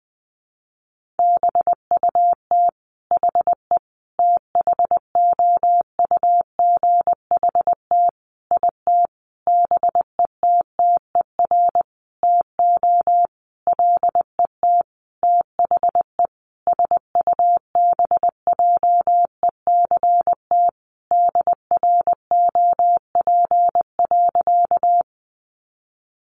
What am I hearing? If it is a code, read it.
Morse code, heard as a keyed tone, about 20 wpm, 700 Hz. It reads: BUT HE THOUGHT IT BETTER TO LET THE SUBJECT DROP.
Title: Sentences from Top 1400 Words 20wpm